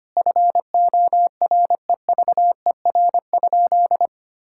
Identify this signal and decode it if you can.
Morse code, heard as a keyed tone, about 25 wpm, 700 Hz. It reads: FOREVER?
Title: Sentences from Top 1600 Words 25wpm